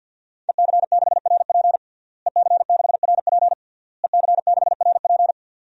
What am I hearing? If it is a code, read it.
Morse code, heard as a keyed tone, about 53 wpm, 700 Hz. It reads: EC6RP EC6RP EC6RP
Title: DXCC-100 Call Signs - 5 Characters - Repeated 3x - 50wpm